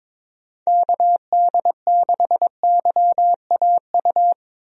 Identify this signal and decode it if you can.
Morse code, heard as a keyed tone, about 22 wpm, 700 Hz. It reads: KD6YAU
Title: USA Call Signs - 6 Characters - 22wpm